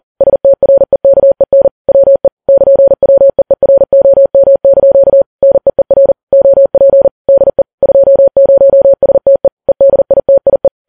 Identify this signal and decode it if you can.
Morse code, heard as a keyed tone, about 40 wpm, 550 Hz. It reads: STREKEN WE <KN>WEEROM) NEER OP DE 20STE EDITIE
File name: Files Web tls / Morse on9fv / morse40kps